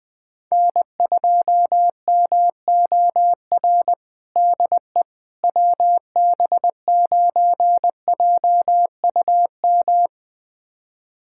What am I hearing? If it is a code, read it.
Morse code, heard as a keyed tone, about 20 wpm, 700 Hz. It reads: N2MOR DE WB9JUM
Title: USA Callsign DE Callsign 20wpm